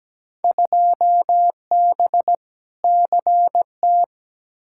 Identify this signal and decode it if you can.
Morse code, heard as a keyed tone, about 17 wpm, 700 Hz. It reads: 2B CT
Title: Field Day Exchanges Encore - 17wpm